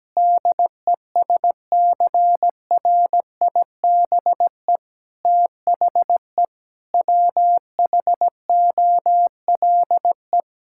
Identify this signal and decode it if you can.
Morse code, heard as a keyed tone, about 17 wpm, 700 Hz. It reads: DESCRIBE THE WHOLE
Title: Sentences from Top 1600 Words 17wpm